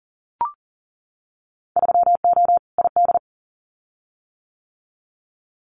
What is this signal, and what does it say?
Morse code, heard as a keyed tone, about 41 wpm, 700 Hz. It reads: 3O ID
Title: Winter Field Day Exchanges Encore 40wpm